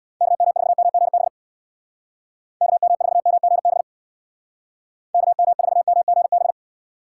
Morse code, heard as a keyed tone, about 60 wpm, 700 Hz. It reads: KM4MGB KM4MGB KM4MGB
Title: Speed Racing - USA 2x3 Call Signs 50wpm